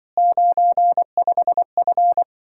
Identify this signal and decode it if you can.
Morse code, heard as a keyed tone, about 24 wpm, 700 Hz. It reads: 95F
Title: Speed-Racing Common QSO Phrases Encore - 20wpm